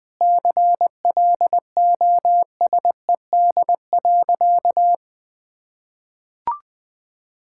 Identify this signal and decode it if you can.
Morse code, heard as a keyed tone, about 20 wpm, 700 Hz. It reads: CLOSED.
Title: Sentences with 7-Word Phrases - 20wpm